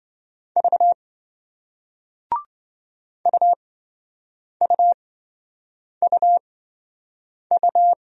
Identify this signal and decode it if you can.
Morse code, heard as a keyed tone, about 28 wpm, spaced as for 7 wpm, 700 Hz. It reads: V UUUU